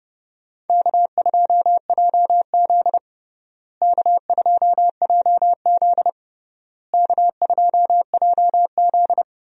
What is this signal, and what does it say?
Morse code, heard as a keyed tone, about 32 wpm, 700 Hz. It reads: K2JZ K2JZ K2JZ
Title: USA Call Signs - 4 Characters - Repeated 3x - 30wpm